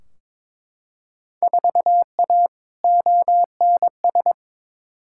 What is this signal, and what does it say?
Morse code, heard as a keyed tone, about 22 wpm, 700 Hz. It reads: 4A ONS
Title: Field Day Exchanges - 22wpm